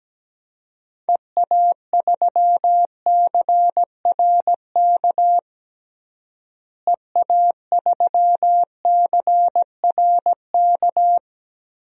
Morse code, heard as a keyed tone, about 17 wpm, 700 Hz. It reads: EA3CRK EA3CRK
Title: DXCC-100 Call Signs - 6 Characters - Repeated 2x - 17wpm